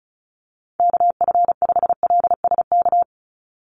Morse code, heard as a keyed tone, about 34 wpm, 700 Hz. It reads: KF5LSK
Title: USA Call Signs 35wpm